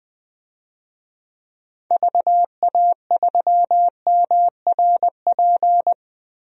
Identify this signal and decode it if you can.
Morse code, heard as a keyed tone, about 20 wpm, 700 Hz. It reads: VA3MRP
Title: DXCC-100 Call Signs - 6 Characters - Repeated 2x - 20wpm